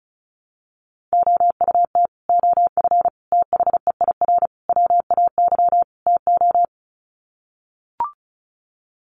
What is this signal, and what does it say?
Morse code, heard as a keyed tone, about 34 wpm, 700 Hz. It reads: OUT OF THEIR WAY TO
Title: Sets of 5 Words Encore2 35wpm